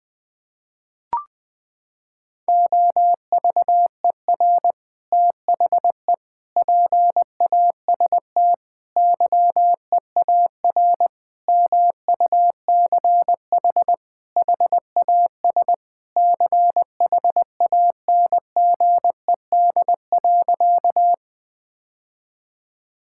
Morse code, heard as a keyed tone, about 20 wpm, 700 Hz. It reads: OVER THE PAST YEAR MUCH HAS CHANGED.
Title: Sentences with 3-Word Phrases - 20wpm